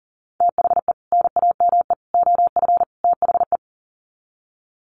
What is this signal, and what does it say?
Morse code, heard as a keyed tone, about 40 wpm, 700 Hz. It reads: THE NAME OF THE
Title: Sets of 4 Words 40wpm